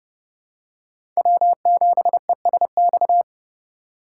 Morse code, heard as a keyed tone, about 32 wpm, 700 Hz. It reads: W7ESX
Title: USA Call Signs - 5 Characters - 4x Character Spacing - 30wpm